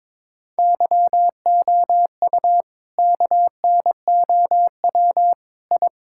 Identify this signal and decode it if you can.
Morse code, heard as a keyed tone, about 22 wpm, 700 Hz. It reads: YOU KNOW I
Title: Sentences from Top 1000 Words - Review Part 6 - 22wpm